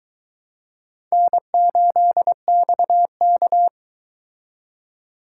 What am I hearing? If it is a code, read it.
Morse code, heard as a keyed tone, about 22 wpm, 700 Hz. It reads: N8XK